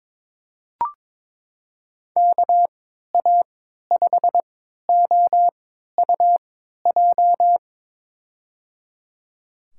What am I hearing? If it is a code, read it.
Morse code, heard as a keyed tone, about 21 wpm, spaced as for 12 wpm, 700 Hz. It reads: KA5OUJ
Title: USA Call Signs - 6 Characters - 3x Character Spacing - 22wpm